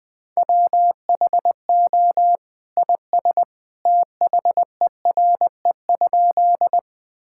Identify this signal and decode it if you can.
Morse code, heard as a keyed tone, about 20 wpm, 700 Hz. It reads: WHO IS THERE?
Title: Sentences with 2-Word Phrases - Spoken Beginning - 20wpm